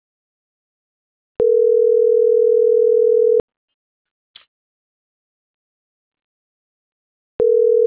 Ringback tone (North America), 440 + 480 Hz.